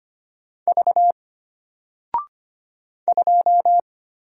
Morse code, heard as a keyed tone, about 25 wpm, 700 Hz. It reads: V 2
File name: Single-Letter-Number-25wpm